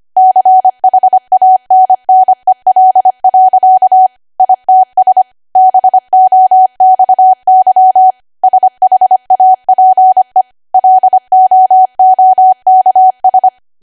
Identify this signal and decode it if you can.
Morse code, heard as a keyed tone, about 25 wpm, 750 Hz. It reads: CHANNEL. ITS BOXY SHAPE LOOKS
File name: cwa / W1AW-Level3 / 25WPM